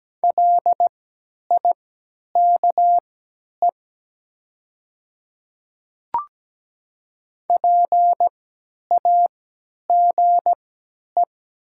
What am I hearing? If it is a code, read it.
Morse code, heard as a keyed tone, about 17 wpm, spaced as for 10 wpm, 690 Hz. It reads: LIKE PAGE
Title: 3x Character Spacing - Top 300 Words - 17wpm